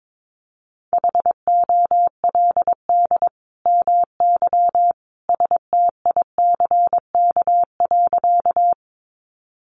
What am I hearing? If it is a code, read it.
Morse code, heard as a keyed tone, about 22 wpm, 700 Hz. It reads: HOLD MY STICK.
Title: Sentences from Top 1200 Words 22wpm